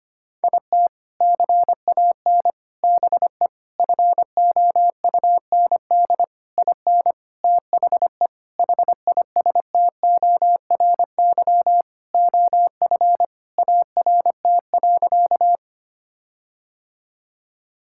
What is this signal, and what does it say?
Morse code, heard as a keyed tone, about 25 wpm, 700 Hz. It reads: IT CAN BE FOUND IN THE HISTORY OF ART.
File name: practice / 25wpm / Sentences with 4-Word Phrases - 25wpm